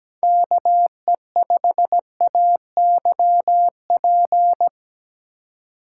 Morse code, heard as a keyed tone, about 17 wpm, 700 Hz. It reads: KE5AYP
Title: USA Call Signs - 6 Characters - 2x Character Spacing - 17wpm